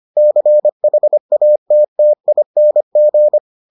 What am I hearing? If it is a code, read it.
Morse code, heard as a keyed tone, about 25 wpm, 600 Hz. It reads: CHATTING